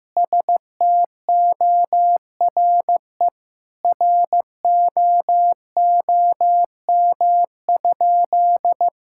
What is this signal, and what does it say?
Morse code, heard as a keyed tone, about 15 wpm, 700 Hz. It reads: STORE ROOM?
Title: Sentences from Top 1000 Words - Review Part 5 - 15wpm